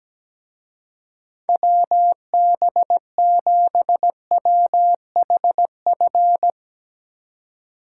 Morse code, heard as a keyed tone, about 18 wpm, 700 Hz. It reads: WB7WHF
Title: Speed Racing - USA 2x3 Call Signs 17wpm